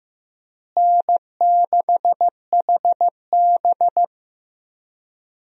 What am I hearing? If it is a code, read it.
Morse code, heard as a keyed tone, about 15 wpm, 700 Hz. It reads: N6HB
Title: USA Call Signs - 4 Characters - Repeated 2x - 15wpm